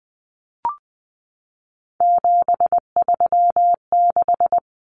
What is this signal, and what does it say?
Morse code, heard as a keyed tone, about 20 wpm, 700 Hz. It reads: 736
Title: Common Cut Numbers - 3-Digit 20wpm